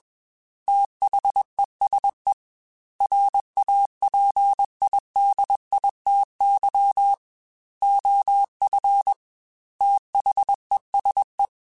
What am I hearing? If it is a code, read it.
Morse code, heard as a keyed tone, about 21 wpm, 800 Hz. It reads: THESE RAPIDITY OF THESE
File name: Technical Subjects / Technical Subjects-mp3s / Reactance @20